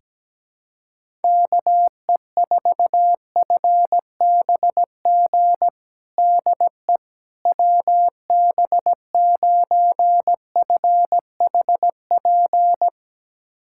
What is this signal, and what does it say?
Morse code, heard as a keyed tone, about 17 wpm, 700 Hz. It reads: KE4FBG DE WB9FHP
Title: USA Callsign DE Callsign - 6 Characters - 17wpm